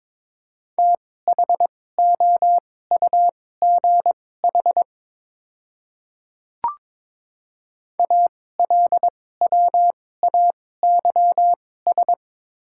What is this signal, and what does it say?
Morse code, heard as a keyed tone, about 22 wpm, spaced as for 16 wpm, 700 Hz. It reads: THOUGH ALWAYS
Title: 2x Character Spacing - Top 500 Words - 22wpm